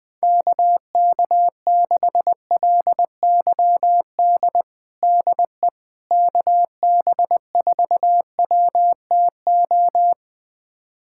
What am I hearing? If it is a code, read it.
Morse code, heard as a keyed tone, about 20 wpm, 700 Hz. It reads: KK6LYD DE KB4WTO
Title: USA Callsign DE Callsign - 6 Characters - 20wpm